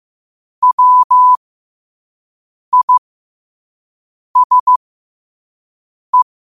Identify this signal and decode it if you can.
Morse code, heard as a keyed tone, about 15 wpm, spaced as for 5 wpm, 1000 Hz. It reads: WISE
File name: qotdfiles / 5wpm